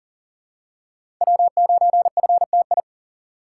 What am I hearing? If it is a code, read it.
Morse code, heard as a keyed tone, about 40 wpm, 700 Hz. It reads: W9FTI
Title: USA Call Signs - 8x Character Spacing - 40wpm